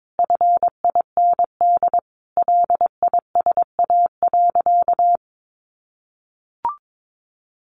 Morse code, heard as a keyed tone, about 22 wpm, 700 Hz. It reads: FIND LISA.
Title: Additional Easy Phrases 22wpm